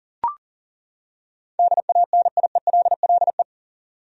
Morse code, heard as a keyed tone, about 40 wpm, 700 Hz. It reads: DANIELLE
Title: USA Common Names - 40wpm